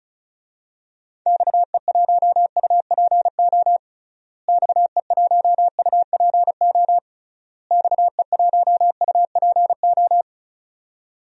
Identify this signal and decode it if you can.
Morse code, heard as a keyed tone, about 36 wpm, 700 Hz. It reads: XE1UPO XE1UPO XE1UPO
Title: DXCC-100 Call Signs - 6 Characters - Repeated 3x - 35wpm